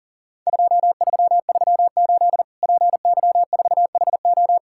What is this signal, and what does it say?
Morse code, heard as a keyed tone, about 37 wpm, 700 Hz. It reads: 2338 PY4HO
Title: Worked All Europe DX Contest - QTC Exchange - 40wpm